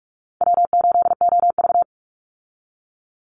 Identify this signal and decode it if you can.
Morse code, heard as a keyed tone, about 44 wpm, 700 Hz. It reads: W8OV
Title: Easy Call Signs - 45wpm